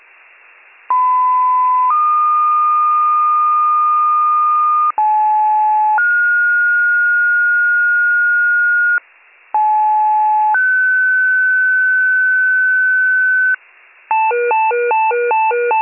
Two-tone page, heard as a one-second tone, then a three-second tone, about 1010 then 1220 Hz, about 860 then 1430 Hz, about 860 then 1500 Hz, with noise. It is heard from 0.9 to 13.5 s.